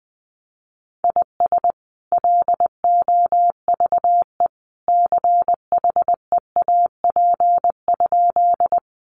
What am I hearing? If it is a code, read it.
Morse code, heard as a keyed tone, about 20 wpm, 700 Hz. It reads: IS LOVE CHEAP?
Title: Sentences from Top 1200 Words 20wpm